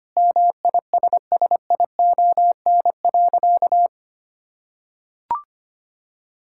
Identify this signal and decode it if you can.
Morse code, heard as a keyed tone, about 25 wpm, 700 Hz. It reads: MISSION.
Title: Sentences from Top 1600 Words 25wpm